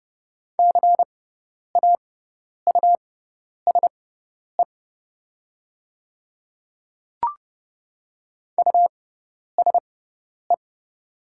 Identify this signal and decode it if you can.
Morse code, heard as a keyed tone, about 30 wpm, spaced as for 10 wpm, 700 Hz. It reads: CAUSE USE
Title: 6x Character Spacing - Top 500 Words - 30wpm